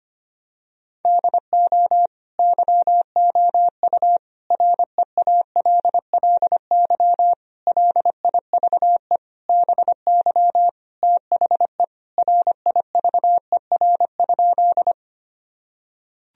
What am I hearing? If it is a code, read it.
Morse code, heard as a keyed tone, about 25 wpm, 700 Hz. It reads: DO YOU REALLY LIVE BY THE RIVER?
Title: Sentences from Top 1000 Words - Review Part 1 - 25wpm